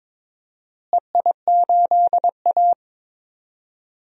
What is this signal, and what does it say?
Morse code, heard as a keyed tone, about 22 wpm, 700 Hz. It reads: EI8A
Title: DXCC-100 Call Signs - 4 Characters - Repeated 2x - 22wpm